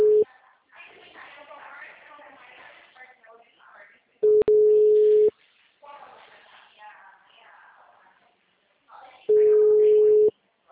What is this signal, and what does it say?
Ringback tone (Europe, 425 Hz), with speech.